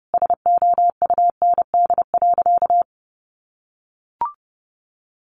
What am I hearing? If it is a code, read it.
Morse code, heard as a keyed tone, about 30 wpm, 700 Hz. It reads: SOUND.
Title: Sentences from Top 1200 Words 30wpm